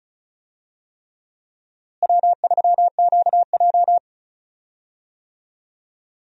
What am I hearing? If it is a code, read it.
Morse code, heard as a keyed tone, about 36 wpm, 700 Hz. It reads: W3QJ
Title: USA Call Signs - 4 Characters - Repeated 3x - 35wpm